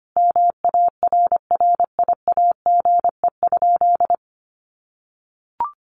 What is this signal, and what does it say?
Morse code, heard as a keyed tone, about 25 wpm, 700 Hz. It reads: MARRIAGE?
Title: Sentences from Top 1200 Words 25wpm